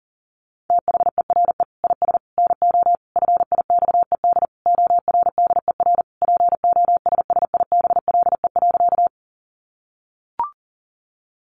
Morse code, heard as a keyed tone, about 37 wpm, 700 Hz. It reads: THERE IS NO FIXED ORDER POSSIBLE.
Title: Sentences from Top 1100 Words 40wpm